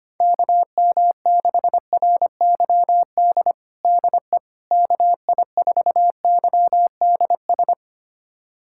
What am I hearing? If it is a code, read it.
Morse code, heard as a keyed tone, about 25 wpm, 700 Hz. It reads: KM6RYD DE KI4YDS
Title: USA Callsign DE Callsign - 6 Characters - 25wpm